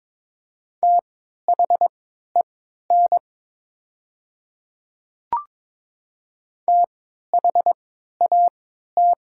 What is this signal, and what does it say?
Morse code, heard as a keyed tone, about 22 wpm, spaced as for 13 wpm, 700 Hz. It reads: THEN THAT
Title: 3x Character Spacing - Top 100 Words - 22wpm